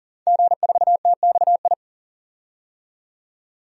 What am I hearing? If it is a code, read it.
Morse code, heard as a keyed tone, about 40 wpm, 700 Hz. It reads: G4TXI
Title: DXCC-100 Call Signs - 5 Characters - 40wpm